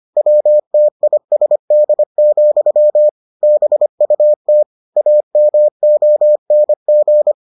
Morse code, heard as a keyed tone, about 25 wpm, 600 Hz. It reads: WTISD, BUT AMONG